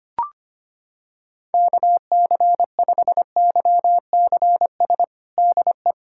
Morse code, heard as a keyed tone, about 25 wpm, 700 Hz. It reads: KC5YCS DE
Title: USA Callsign DE Callsign - 6 Characters - 25wpm